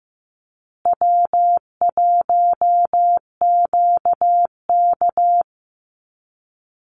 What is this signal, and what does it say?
Morse code, heard as a keyed tone, about 14 wpm, 700 Hz. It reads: W1QK
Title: Easy Call Signs Repeated 3x - 15wpm